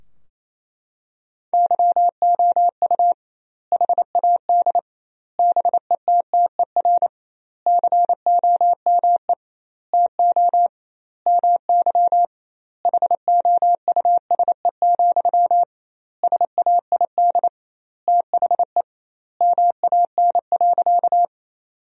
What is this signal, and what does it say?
Morse code, heard as a keyed tone, about 28 wpm, 700 Hz. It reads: YOU HAD BETTER COME TO MY HOUSE, SAID THE MAN.
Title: 2x Word-Spacing - Sentences from Top 400 Words - 28wpm